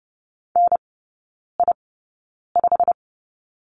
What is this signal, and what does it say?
Morse code, heard as a keyed tone, about 30 wpm, spaced as for 9 wpm, 700 Hz. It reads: NI5
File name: course / 30:10wpm / 087 Lesson 15e - Number 5 - Three Characters 30wpm